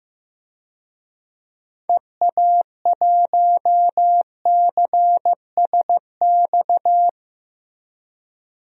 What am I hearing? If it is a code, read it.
Morse code, heard as a keyed tone, about 16 wpm, 700 Hz. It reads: EA1CSX
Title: DXCC-100 Call Signs - 6 Characters - 15wpm